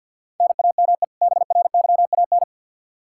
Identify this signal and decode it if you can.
Morse code, heard as a keyed tone, about 52 wpm, 700 Hz. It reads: NAME BRYAN
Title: K1USN SST QSO Practice 50wpm